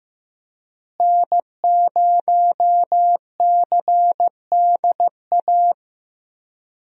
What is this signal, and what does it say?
Morse code, heard as a keyed tone, about 15 wpm, 700 Hz. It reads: N0CDA
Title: USA Call Signs - 5 Characters - Repeated 3x - 15wpm